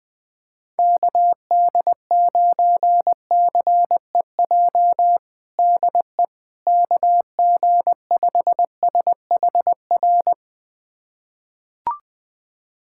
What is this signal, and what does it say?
Morse code, heard as a keyed tone, about 20 wpm, 700 Hz. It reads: KD9CEJ DE KG5SHR
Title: USA Callsign DE Callsign - 6 Characters - 20wpm